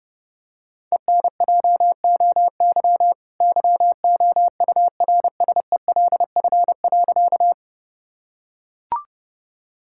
Morse code, heard as a keyed tone, about 30 wpm, 700 Hz. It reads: ENJOY YOURSELF.
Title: Sentences from Top 1000 Words - Review Part 5 - 30wpm